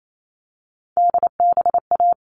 Morse code, heard as a keyed tone, about 28 wpm, 700 Hz. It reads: DBA
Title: All Cut Numbers - 3-Digit 28wpm